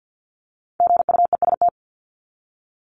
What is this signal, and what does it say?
Morse code, heard as a keyed tone, about 50 wpm, 700 Hz. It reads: GUEST
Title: Top 1501-1750 Words 50wpm